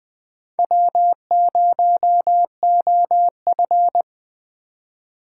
Morse code, heard as a keyed tone, about 21 wpm, 700 Hz. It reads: W0OF
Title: Speed Racing - USA Call Signs 17wpm